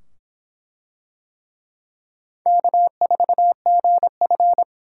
Morse code, heard as a keyed tone, about 23 wpm, 700 Hz. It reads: K4GF